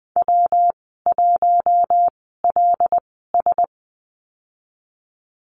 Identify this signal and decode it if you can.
Morse code, heard as a keyed tone, about 20 wpm, spaced as for 15 wpm, 700 Hz. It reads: W1LS
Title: 167 Lesson 27g - Number 7 - Call Signs 20wpm